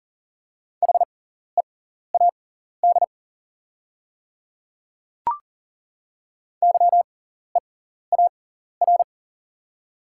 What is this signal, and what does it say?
Morse code, heard as a keyed tone, about 41 wpm, spaced as for 14 wpm, 690 Hz. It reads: HEAD YEAR
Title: 6x Character Spacing - Top 500 Words - 40wpm